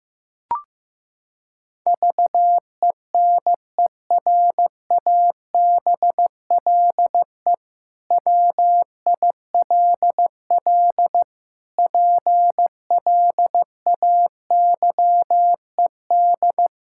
Morse code, heard as a keyed tone, about 15 wpm, 700 Hz. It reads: VENERABLE WILL PLAYED
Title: Pangram Challenge v2 - 15wpm